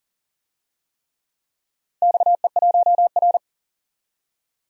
Morse code, heard as a keyed tone, about 39 wpm, 700 Hz. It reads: XE1R
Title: DXCC-100 Call Signs - 4 Characters - 40wpm